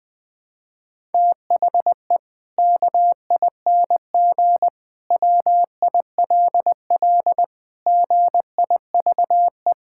Morse code, heard as a keyed tone, about 19 wpm, 700 Hz. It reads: THE KING WILL GIVE
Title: Sentences from Top 1000 Words - Review Part 5 - 20wpm